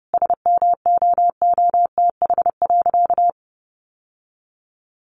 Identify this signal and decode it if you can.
Morse code, heard as a keyed tone, about 30 wpm, 700 Hz. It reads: SMOOTH.
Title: Sentences from Top 1400 Words 30wpm